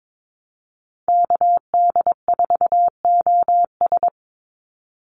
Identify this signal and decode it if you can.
Morse code, heard as a keyed tone, about 22 wpm, 700 Hz. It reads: KD4OS
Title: Call Signs Repeated 3x - 22wpm